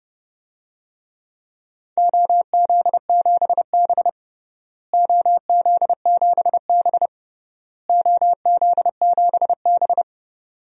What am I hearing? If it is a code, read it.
Morse code, heard as a keyed tone, about 30 wpm, 700 Hz. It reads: OZ7B OZ7B OZ7B
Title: DXCC-100 Call Signs - 4 Characters - Repeated 3x - 30wpm